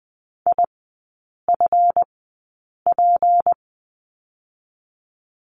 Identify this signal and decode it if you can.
Morse code, heard as a keyed tone, about 20 wpm, spaced as for 8 wpm, 700 Hz. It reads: IFP
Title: 166 Lesson 27e - Number 7 - Three Characters 20wpm